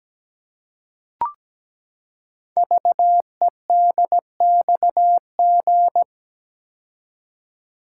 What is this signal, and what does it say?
Morse code, heard as a keyed tone, about 17 wpm, 700 Hz. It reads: VEDXG
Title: Code Groups - 5 Letters - 17wpm